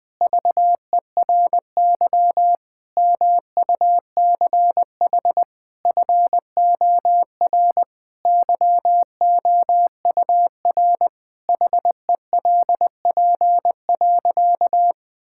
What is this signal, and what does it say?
Morse code, heard as a keyed tone, about 21 wpm, 700 Hz. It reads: VERY MUCH FOR YOUR HELP.
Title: Sentences with 5-Word Phrases - 20wpm